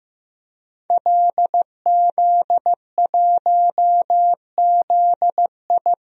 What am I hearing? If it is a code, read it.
Morse code, heard as a keyed tone, about 15 wpm, 700 Hz. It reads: LZ1ZI
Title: DXCC-100 Call Signs - 5 Characters - 15wpm